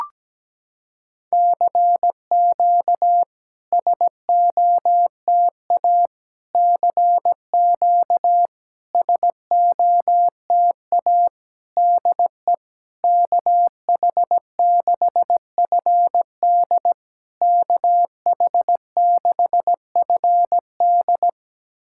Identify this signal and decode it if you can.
Morse code, heard as a keyed tone, about 17 wpm, 700 Hz. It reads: CQ SOTA CQ SOTA DE KH6FD KH6FD
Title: SOTA QSO Practice 17wpm